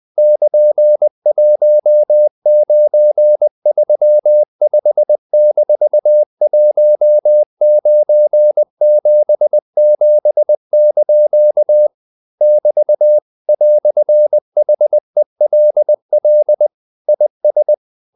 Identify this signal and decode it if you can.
Morse code, heard as a keyed tone, about 20 wpm, 600 Hz. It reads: <KN>1935-1977) <BT> "HELL IS
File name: CW / Practice / EnglishQuotations0320